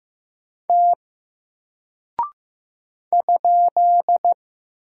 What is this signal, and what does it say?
Morse code, heard as a keyed tone, about 15 wpm, 700 Hz. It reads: T ?
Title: Single Character (Letter:Number:Punctuation:Prosign) Alt 15wpm